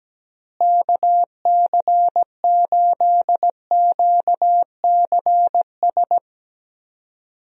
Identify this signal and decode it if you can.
Morse code, heard as a keyed tone, about 17 wpm, 700 Hz. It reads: KC8QCS
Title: USA Call Signs - 6 Characters - Repeated 3x - 17wpm